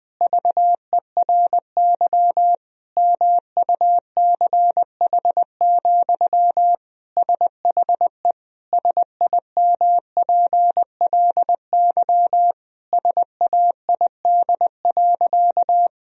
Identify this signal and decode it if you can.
Morse code, heard as a keyed tone, about 21 wpm, 700 Hz. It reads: VERY MUCH, SHE SIMPLY SAID.
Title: Sentences from Top 1000 Words - Review Part 4 - 20wpm